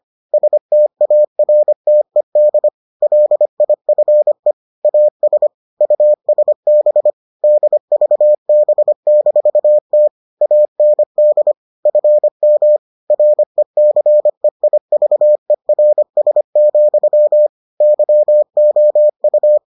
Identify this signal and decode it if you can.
Morse code, heard as a keyed tone, about 25 wpm, 600 Hz. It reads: STARTED LIFE AS USB DVB-T AND FM RECEIVERS, YOU